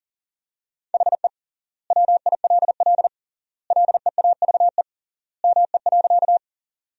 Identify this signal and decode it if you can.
Morse code, heard as a keyed tone, about 40 wpm, 700 Hz. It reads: HE WILL LEAVE ME.
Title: 3x Word-Spacing - Sentences from Top 400 Words - 40wpm